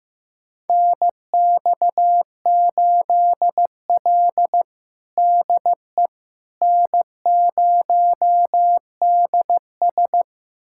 Morse code, heard as a keyed tone, about 16 wpm, 700 Hz. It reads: NX8L DE N0DS
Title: USA Callsign DE Callsign - 4 Characters - 15wpm